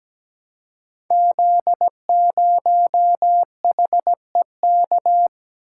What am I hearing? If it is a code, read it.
Morse code, heard as a keyed tone, about 17 wpm, 700 Hz. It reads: Z0HEK
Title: Code Groups - 5 Letter:Numbers - 17wpm